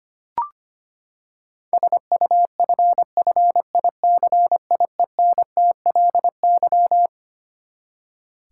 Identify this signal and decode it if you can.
Morse code, heard as a keyed tone, about 25 wpm, 700 Hz. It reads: SUFFICIENTLY
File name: Twelve Letter Words - 25wpm